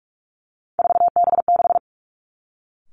Morse code, heard as a keyed tone, about 46 wpm, 700 Hz. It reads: 4B6
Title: Three Challenging Letter-Number Combinations 45wpm